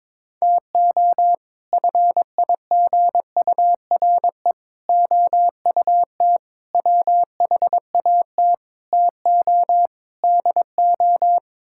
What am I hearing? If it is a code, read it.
Morse code, heard as a keyed tone, about 22 wpm, 700 Hz. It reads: TO FIGURE OUT WHAT TO DO
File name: practice / 22wpm / Sets of 6 Words - 22wpm